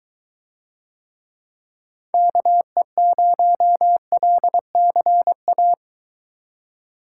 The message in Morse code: KE0LCA

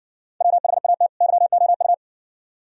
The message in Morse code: WHAT YOU

